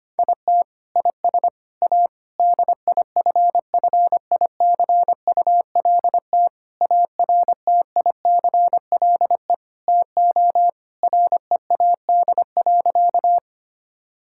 Morse code, heard as a keyed tone, about 25 wpm, 700 Hz. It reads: IT IS A DIFFICULT ARTICLE TO READ.
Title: Sentences from Top 1000 Words - Review Part 5 - 25wpm